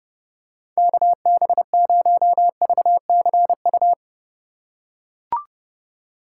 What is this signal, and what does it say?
Morse code, heard as a keyed tone, about 30 wpm, 700 Hz. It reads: KB0VCU